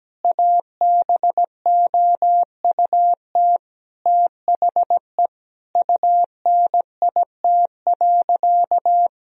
Morse code, heard as a keyed tone, about 17 wpm, 700 Hz. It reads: ABOUT THE UNIT.
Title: Sentences from Top 1000 Words - Review Part 2 - 17wpm